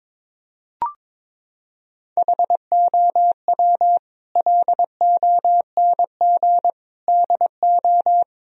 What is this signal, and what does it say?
Morse code, heard as a keyed tone, about 22 wpm, 700 Hz. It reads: HOW LONG DO
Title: Sentences from Top 1000 Words - Review Part 2 - 22wpm